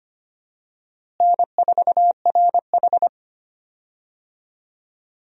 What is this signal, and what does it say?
Morse code, heard as a keyed tone, about 25 wpm, 700 Hz. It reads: N4RH
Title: USA Call Signs - 4 Characters - Repeated 2x - 25wpm